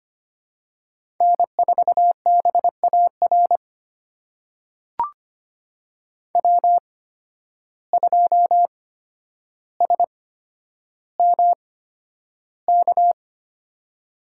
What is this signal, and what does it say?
Morse code, heard as a keyed tone, about 25 wpm, spaced as for 11 wpm, 700 Hz. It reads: N4BAR W2SMK
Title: USA Call Signs - 5 Characters - 8x Character Spacing - 25wpm